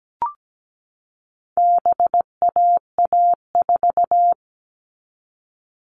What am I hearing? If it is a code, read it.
Morse code, heard as a keyed tone, about 17 wpm, 700 Hz. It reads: BAA4
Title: All Cut Numbers - 4-Digit 17wpm